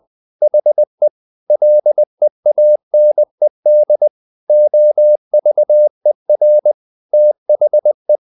Morse code, heard as a keyed tone, about 20 wpm, 600 Hz. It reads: HE LEANED OVER THE
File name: the old man and the sea part 2 20wpm